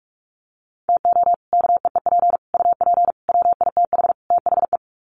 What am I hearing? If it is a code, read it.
Morse code, heard as a keyed tone, about 44 wpm, 700 Hz. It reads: TO KEEP UP WITH THE
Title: Sets of 5 Words Encore 45wpm